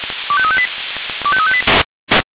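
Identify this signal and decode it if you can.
Five-tone selective call (ZVEI), heard as quick runs of five tones, tones of about 70 ms. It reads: 25358 25358
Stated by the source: ZVEI five-tone selective call received on an SDR.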